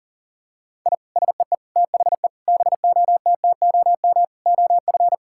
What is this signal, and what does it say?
Morse code, heard as a keyed tone, about 40 wpm, 700 Hz. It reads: I SEE THE BOTTOM OF
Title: Sentences from Top 1000 Words - Review Part 6 - 40wpm